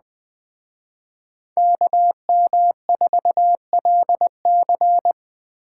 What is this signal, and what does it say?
Morse code, heard as a keyed tone, about 20 wpm, 700 Hz. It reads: KM4LC
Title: Speed Racing - USA 2x2 Call Signs 17wpm